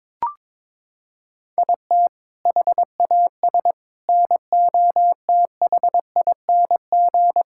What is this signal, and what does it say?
Morse code, heard as a keyed tone, about 22 wpm, 700 Hz. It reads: IT HAS NOTHING
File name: practice / 22wpm / Sentences with 4-Word Phrases - 22wpm